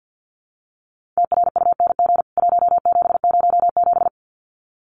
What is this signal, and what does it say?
Morse code, heard as a keyed tone, about 50 wpm, 700 Hz. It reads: TRUNG 1707
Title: CWT Exchange 50wpm